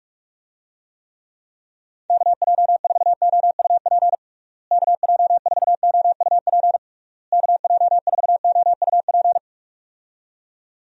Morse code, heard as a keyed tone, about 46 wpm, 700 Hz. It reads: KJ4OUP KJ4OUP KJ4OUP
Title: USA Call Signs - 6 Characters - Repeated 3x - 45wpm